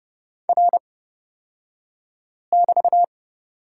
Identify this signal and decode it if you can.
Morse code, heard as a keyed tone, about 30 wpm, spaced as for 5 wpm, 700 Hz. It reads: R<BT>
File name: Lesson 38d - Letter q - Two Characters 30wpm